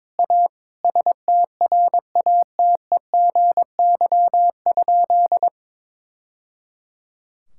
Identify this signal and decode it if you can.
Morse code, heard as a keyed tone, about 21 wpm, 700 Hz. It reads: A STRATEGY?
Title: Sentences from Top 1000 Words - Review Part 5 - 22wpm